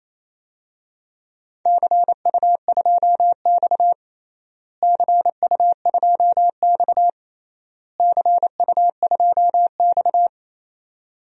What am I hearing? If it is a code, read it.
Morse code, heard as a keyed tone, about 27 wpm, 700 Hz. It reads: CU2X CU2X CU2X
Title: DXCC-100 Call Signs - 4 Characters - Repeated 3x - 28wpm